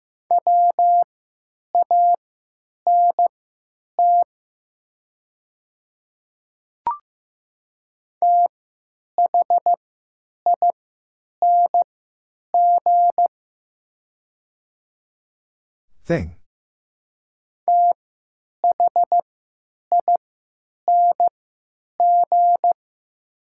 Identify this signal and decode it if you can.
Morse code, heard as a keyed tone, about 15 wpm, spaced as for 9 wpm, 700 Hz. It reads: WANT THING THING
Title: 3x Character Spacing - Top 200 Words - 15wpm